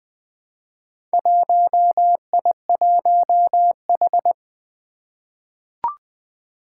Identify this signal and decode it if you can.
Morse code, heard as a keyed tone, about 20 wpm, 700 Hz. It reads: 1I1H
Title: Four Challenging Letter-Number Combinations 20wpm